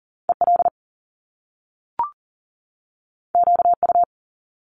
Morse code, heard as a keyed tone, about 39 wpm, 700 Hz. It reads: EL QU
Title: Top Two Letter Combinations in Words 40wpm